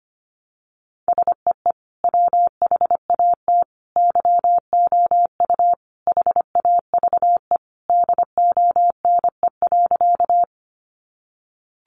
Morse code, heard as a keyed tone, about 25 wpm, 700 Hz. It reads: SEE WHAT YOU HAVE DONE.
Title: Sentences from Top 500 Words 25wpm